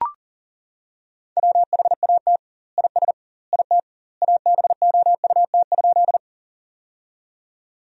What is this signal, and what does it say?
Morse code, heard as a keyed tone, about 37 wpm, 700 Hz. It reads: WHAT IS IT ABOUT?
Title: 2x Word-Spacing - Sentences from Top 100 Words - 40wpm